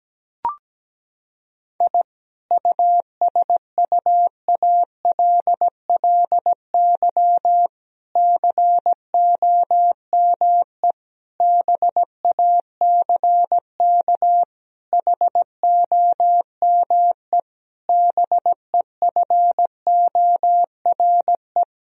Morse code, heard as a keyed tone, about 17 wpm, 700 Hz. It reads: I USUALLY COME BACK HOME BEFORE
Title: Sentences from Top 1000 Words - Review Part 3 - 17wpm